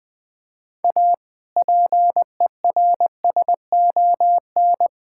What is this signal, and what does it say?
Morse code, heard as a keyed tone, about 20 wpm, 700 Hz. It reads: A PERSON
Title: Sentences with 6-Word Phrases - 20wpm